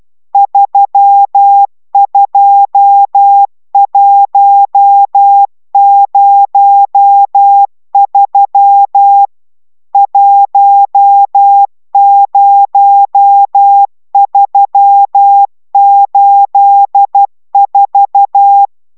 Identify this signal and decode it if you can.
Morse code, heard as a keyed tone, about 12 wpm, 800 Hz. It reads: 32103 10384